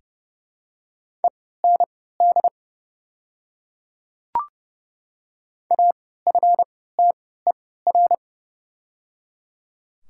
Morse code, heard as a keyed tone, about 30 wpm, spaced as for 17 wpm, 700 Hz. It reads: END AFTER
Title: 3x Character Spacing - Top 300 Words - 30wpm